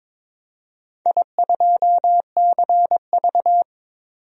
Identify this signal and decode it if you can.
Morse code, heard as a keyed tone, about 22 wpm, 700 Hz. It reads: I2CV
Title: DXCC-100 Call Signs - 4 Characters - 22wpm